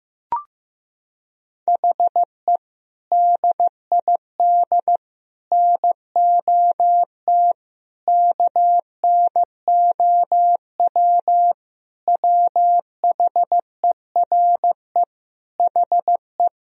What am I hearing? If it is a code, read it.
Morse code, heard as a keyed tone, about 16 wpm, 700 Hz. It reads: HE DID NOT KNOW WHERE HE
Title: Sentences from Top 1000 Words - Review Part 2 - 15wpm